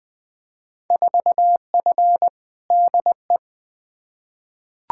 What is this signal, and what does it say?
Morse code, heard as a keyed tone, about 21 wpm, 700 Hz. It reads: 4F DE
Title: Field Day Exchanges Encore - 20wpm